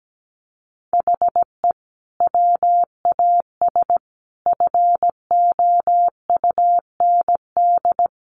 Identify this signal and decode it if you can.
Morse code, heard as a keyed tone, about 17 wpm, 700 Hz. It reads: HE WAS FOUND
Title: Sentences from Top 1500 Words 17wpm